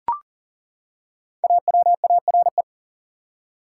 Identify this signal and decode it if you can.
Morse code, heard as a keyed tone, about 40 wpm, 700 Hz. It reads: AWARE
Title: Five Letter Words - 40wpm